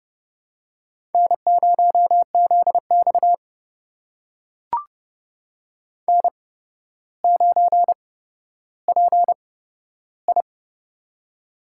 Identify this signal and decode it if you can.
Morse code, heard as a keyed tone, about 30 wpm, spaced as for 13 wpm, 700 Hz. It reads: N0ZX N9PI
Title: USA Call Signs - 4 Characters - 8x Character Spacing - 30wpm